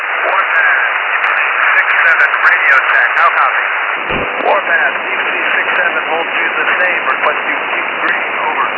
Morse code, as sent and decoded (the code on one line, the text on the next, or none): none